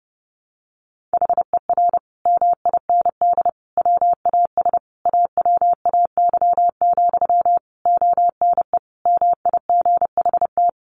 Morse code, heard as a keyed tone, about 28 wpm, 700 Hz. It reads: HER MIND WAS AWAY, ONE MIGHT
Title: Sentences from Top 1500 Words 30wpm